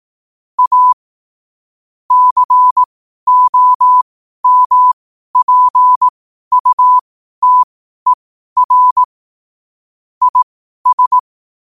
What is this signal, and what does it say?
Morse code, heard as a keyed tone, about 18 wpm, spaced as for 12 wpm, 1000 Hz. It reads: A COMPUTER IS
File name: qotdfiles / 12wpm